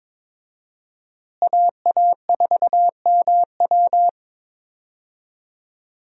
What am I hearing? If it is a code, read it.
Morse code, heard as a keyed tone, about 22 wpm, 700 Hz. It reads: AA4MW
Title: USA Call Signs - 5 Characters - Repeated 2x - 22wpm